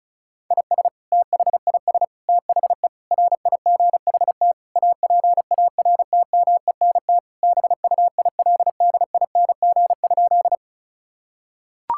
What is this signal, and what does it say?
Morse code, heard as a keyed tone, about 36 wpm, 700 Hz. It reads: IS THIS THE RIGHT APARTMENT BUILDING?
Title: Sentences from Top 1000 Words - Review Part 6 - 35wpm